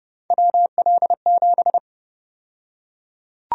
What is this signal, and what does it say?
Morse code, heard as a keyed tone, about 30 wpm, 700 Hz. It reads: WL7
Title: USA Call Sign Prefixes - 30wpm